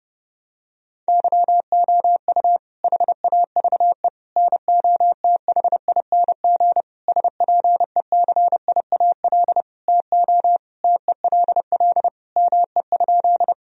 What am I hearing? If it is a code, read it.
Morse code, heard as a keyed tone, about 30 wpm, 700 Hz. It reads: YOU HAVE NOTHING SPECIAL TO TELL ME?
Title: Sentences from Top 1000 Words - Review Part 3 - 30wpm